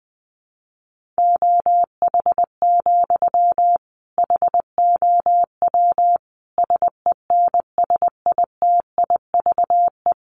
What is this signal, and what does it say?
Morse code, heard as a keyed tone, about 20 wpm, 700 Hz. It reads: OH, HOW SENSITIVE
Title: Sentences from Top 1400 Words 20wpm